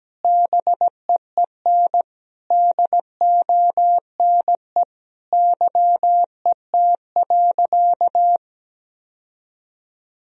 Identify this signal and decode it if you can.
Morse code, heard as a keyed tone, about 17 wpm, 700 Hz. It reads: BEEN DONE YET.
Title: Sentences from Top 1000 Words - Review Part 2 - 17wpm